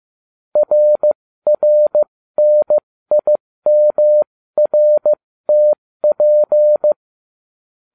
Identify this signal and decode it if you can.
Morse code, heard as a keyed tone, about 15 wpm, spaced as for 13 wpm, 600 Hz. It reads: RRNIMRTP